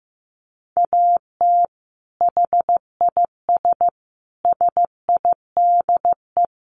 Morse code, heard as a keyed tone, about 15 wpm, 700 Hz. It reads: AT HIS SIDE
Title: Sentences from Top 500 Words 15wpm